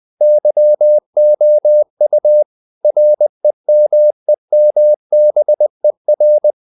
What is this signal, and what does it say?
Morse code, heard as a keyed tone, about 20 wpm, 600 Hz. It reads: YOU REMEMBER
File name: the old man and the sea / the old man and the sea part 1 20wpm